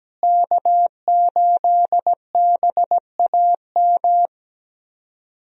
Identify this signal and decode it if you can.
Morse code, heard as a keyed tone, about 17 wpm, 700 Hz. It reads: K8BAM
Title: USA Call Signs - 5 Characters - Repeated 2x - 17wpm